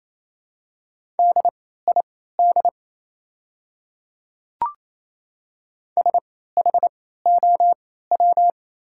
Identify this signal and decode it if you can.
Morse code, heard as a keyed tone, about 28 wpm, spaced as for 16 wpm, 700 Hz. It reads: DID SHOW
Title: 3x Character Spacing - Top 200 Words - 28wpm